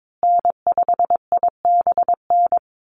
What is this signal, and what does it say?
Morse code, heard as a keyed tone, about 22 wpm, 700 Hz. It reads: N5IBN
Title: USA Call Signs Repeated 2x 22wpm